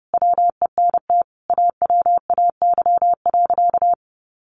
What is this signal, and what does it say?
Morse code, heard as a keyed tone, about 29 wpm, 700 Hz. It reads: WENT AWAY.
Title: Sentences from Top 1100 Words 30wpm